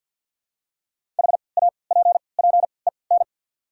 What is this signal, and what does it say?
Morse code, heard as a keyed tone, about 54 wpm, spaced as for 29 wpm, 700 Hz. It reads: HAPPEN
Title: 3x Character Spacing - Top 500 Words - 50wpm